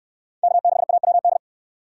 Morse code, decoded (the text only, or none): K6AWD